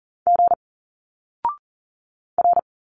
Morse code, heard as a keyed tone, about 40 wpm, 700 Hz. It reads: G R
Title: Single Letters 40wpm